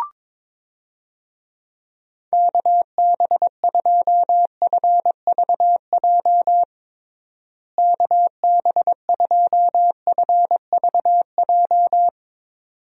Morse code, heard as a keyed tone, about 22 wpm, 700 Hz. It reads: KB2FVJ KB2FVJ
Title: USA Call Signs - 6 Characters - Repeated 2x - 22wpm